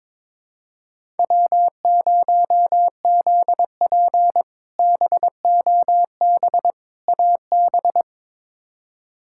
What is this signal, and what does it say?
Morse code, heard as a keyed tone, about 22 wpm, 700 Hz. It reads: W0ZP BOB AB
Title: K1USN SST QSO Practice - 22wpm